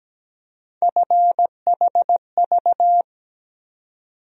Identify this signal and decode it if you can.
Morse code, heard as a keyed tone, about 17 wpm, 700 Hz. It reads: FHV
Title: USA Call Sign Postfixes - 3 Letters 17wpm